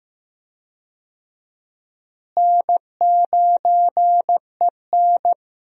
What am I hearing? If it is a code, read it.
Morse code, heard as a keyed tone, about 15 wpm, 700 Hz. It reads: N9EN
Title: USA Call Signs - 4 Characters - Repeated 3x - 15wpm